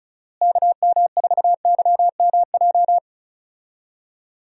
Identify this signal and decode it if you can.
Morse code, heard as a keyed tone, about 36 wpm, 700 Hz. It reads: KM4YMJ
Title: USA Call Signs - 6 Characters - Repeated 2x - 35wpm